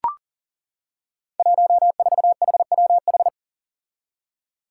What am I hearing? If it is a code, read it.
Morse code, heard as a keyed tone, about 40 wpm, 700 Hz. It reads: 14HWH
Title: Five Challenging Letter-Number Combinations 40wpm